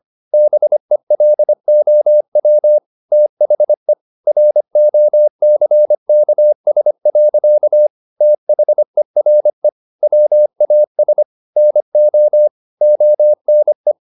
Morse code, heard as a keyed tone, about 25 wpm, 600 Hz. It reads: BELOW THE ROCKS. THERE WAS NO ONE